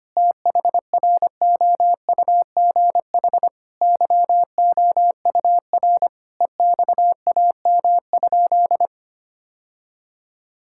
Morse code, heard as a keyed tone, about 25 wpm, 700 Hz. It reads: THROUGH YOUR EXAM?
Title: Sentences from Top 1000 Words - Review Part 5 - 25wpm